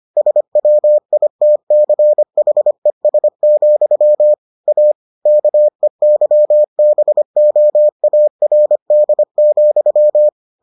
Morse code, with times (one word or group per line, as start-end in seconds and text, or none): 0.2-4.3 SWITCHES,
4.7-4.9 A
5.3-10.3 KEYBOARD,